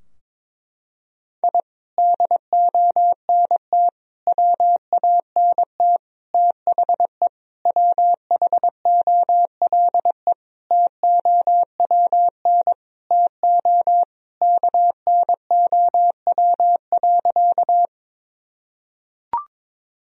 Morse code, heard as a keyed tone, about 22 wpm, 700 Hz. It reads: I DONT WANT THE WHOLE TOWN TO KNOW.
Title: Sentences from Top 1000 Words - Review Part 2 - 22wpm